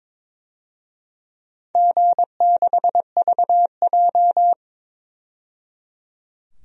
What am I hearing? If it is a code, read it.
Morse code, heard as a keyed tone, about 21 wpm, 700 Hz. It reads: G6VJ
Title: DXCC-100 Call Signs - 4 Characters - 22wpm